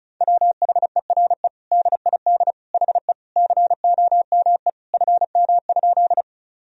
Morse code, heard as a keyed tone, about 36 wpm, 700 Hz. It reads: WHERE DID HE COME FM?
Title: Sentences from Top 200 Words and Abbreviations 35wpm